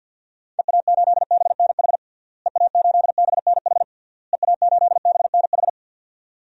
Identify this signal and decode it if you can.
Morse code, heard as a keyed tone, about 51 wpm, 700 Hz. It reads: EA8BNH EA8BNH EA8BNH
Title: DXCC-100 Call Signs - 6 Characters - Repeated 3x - 50wpm